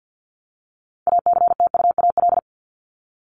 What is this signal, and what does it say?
Morse code, heard as a keyed tone, about 50 wpm, 700 Hz. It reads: ACTUAL
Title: Top 751-1000 Words 50wpm